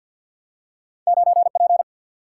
Morse code, heard as a keyed tone, about 48 wpm, 700 Hz. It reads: 9P
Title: Single-Letter Single-Number 50wpm